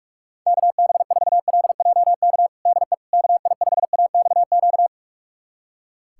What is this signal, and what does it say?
Morse code, heard as a keyed tone, about 45 wpm, 700 Hz. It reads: KB4LJK DE KI5AXQ
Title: USA Callsign DE Callsign - 6 Characters - 45wpm